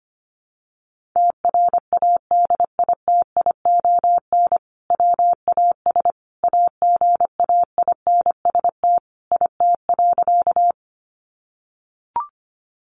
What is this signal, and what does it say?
Morse code, heard as a keyed tone, about 25 wpm, 700 Hz. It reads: TRADITION WAS AGAINST IT.
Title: Sentences from Top 1500 Words 25wpm